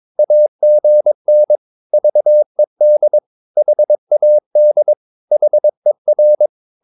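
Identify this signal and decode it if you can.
Morse code, heard as a keyed tone, about 22 wpm, 600 Hz. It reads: AGN VED HAD HER
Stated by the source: Morse practice of short Danish words at 22 wpm, from CW med Gnister.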